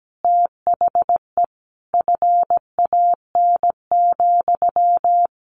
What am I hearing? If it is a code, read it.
Morse code, heard as a keyed tone, about 17 wpm, 700 Hz. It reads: THE FAN,
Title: Sentences from Top 1300 Words 17wpm